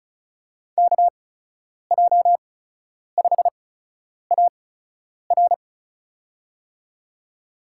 Morse code, heard as a keyed tone, about 36 wpm, spaced as for 10 wpm, 700 Hz. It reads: KJ5AR
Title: USA Call Signs - 8x Character Spacing - 35wpm